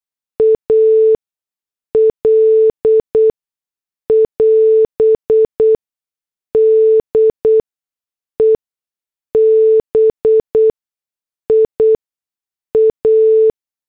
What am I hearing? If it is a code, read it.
Morse code, heard as a keyed tone, about 8 wpm, spaced as for 6 wpm, 440 Hz. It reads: AL <AS> DEBIA